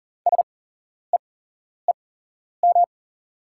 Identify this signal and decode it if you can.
Morse code, heard as a keyed tone, about 39 wpm, spaced as for 11 wpm, 690 Hz. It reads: SEEM
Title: 8x Character Spacing - Top 500 Words - 40wpm